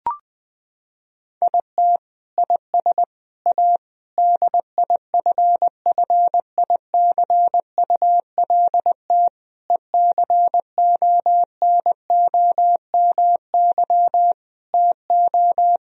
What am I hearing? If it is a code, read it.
Morse code, heard as a keyed tone, about 21 wpm, 700 Hz. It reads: IT IS A DIFFICULT ECONOMY TO
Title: Sentences from Top 1000 Words - Review Part 3 - 20wpm